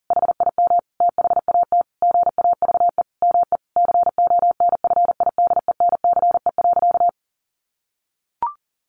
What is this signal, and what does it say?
Morse code, heard as a keyed tone, about 40 wpm, 700 Hz. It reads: HIM THAT GAVE ME CONFIDENCE.
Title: Sentences from Top 1500 Words 40wpm